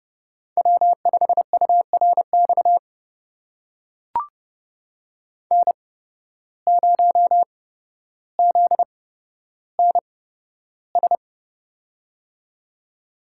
Morse code, heard as a keyed tone, about 29 wpm, spaced as for 13 wpm, 700 Hz. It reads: W5URX N0ZNS
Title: USA Call Signs - 5 Characters - 8x Character Spacing - 30wpm